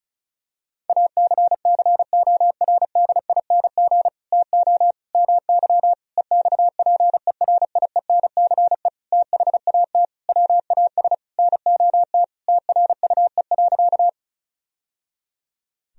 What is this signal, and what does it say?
Morse code, heard as a keyed tone, about 34 wpm, 700 Hz. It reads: ACCORDING TO MY EXPERIENCE THAT WAS NOT TRUE.
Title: Sentences from Top 1000 Words - Review Part 4 - 35wpm